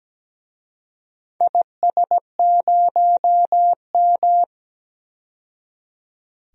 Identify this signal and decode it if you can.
Morse code, heard as a keyed tone, about 17 wpm, 700 Hz. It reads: IS0M
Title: DXCC-100 Call Signs - 4 Characters - 17wpm